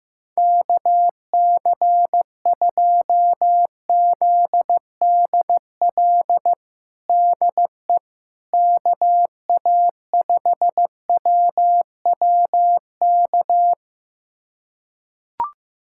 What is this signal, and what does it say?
Morse code, heard as a keyed tone, about 15 wpm, 700 Hz. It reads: KC2ZDL DE KA5WWK
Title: USA Callsign DE Callsign 15wpm